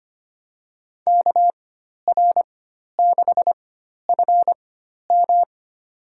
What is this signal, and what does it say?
Morse code, heard as a keyed tone, about 25 wpm, spaced as for 12 wpm, 700 Hz. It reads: KR6FM
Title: USA Call Signs - 4x Character Spacing - 25wpm